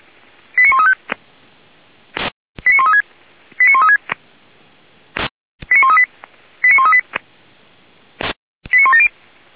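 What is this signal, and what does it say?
Five-tone selective call (ZVEI), heard as quick runs of five tones, tones of about 70 ms. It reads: 89135 89136 89136 89138 89138 89170
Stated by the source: ZVEI five-tone selective call 89134, received via SDR.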